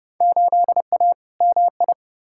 Morse code, heard as a keyed tone, about 30 wpm, 700 Hz. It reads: 8A MI
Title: Field Day Exchanges Encore - 30wpm